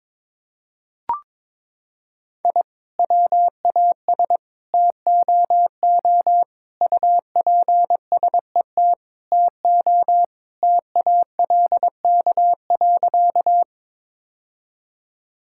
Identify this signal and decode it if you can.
Morse code, heard as a keyed tone, about 22 wpm, 700 Hz. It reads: I WAS TOO UPSET TO TALK.
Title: Sentences from Top 1000 Words - Review Part 5 - 22wpm